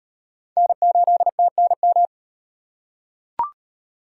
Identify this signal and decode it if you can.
Morse code, heard as a keyed tone, about 35 wpm, 700 Hz. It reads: N8TNM